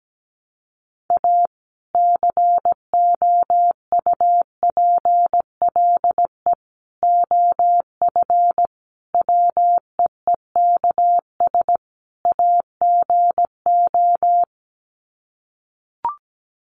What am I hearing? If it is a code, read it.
Morse code, heard as a keyed tone, about 17 wpm, 700 Hz. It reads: A COUPLE OF WEEKS AGO
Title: Sets of 5 Words 17wpm